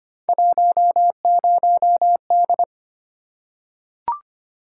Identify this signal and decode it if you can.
Morse code, heard as a keyed tone, about 25 wpm, 700 Hz. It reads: 10D
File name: Three Letter-Number Combinations - 25wpm